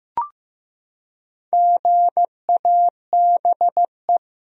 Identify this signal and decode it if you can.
Morse code, heard as a keyed tone, about 15 wpm, 700 Hz. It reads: GABE
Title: USA Common Names - Max 5 Letters - 15wpm